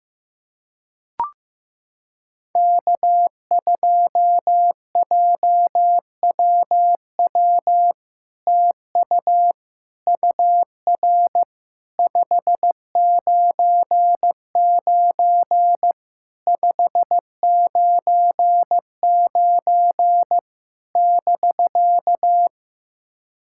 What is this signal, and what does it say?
Morse code, heard as a keyed tone, about 15 wpm, 700 Hz. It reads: K2JWW TU UR 599 599 <BK>
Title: SOTA QSO Practice 15wpm